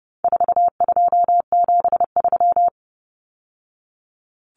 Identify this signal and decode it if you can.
Morse code, heard as a keyed tone, about 30 wpm, 700 Hz. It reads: 4273
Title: Numbers - 4-Digit 30wpm